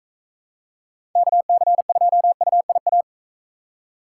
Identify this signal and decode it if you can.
Morse code, heard as a keyed tone, about 42 wpm, 700 Hz. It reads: KC2UIA